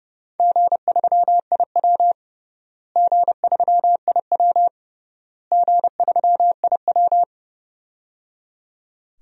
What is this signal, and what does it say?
Morse code, heard as a keyed tone, about 30 wpm, 700 Hz. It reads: G3IW G3IW G3IW
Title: DXCC-100 Call Signs - 4 Characters - Repeated 3x - 30wpm